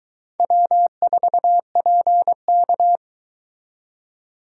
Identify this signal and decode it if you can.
Morse code, heard as a keyed tone, about 22 wpm, 700 Hz. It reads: W4PK